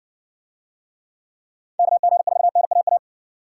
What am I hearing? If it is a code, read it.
Morse code, heard as a keyed tone, about 60 wpm, 700 Hz. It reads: KG4NAA